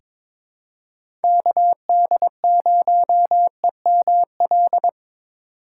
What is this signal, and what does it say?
Morse code, heard as a keyed tone, about 21 wpm, 700 Hz. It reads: KD0EML
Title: USA Call Signs - 4x Character Spacing - 22wpm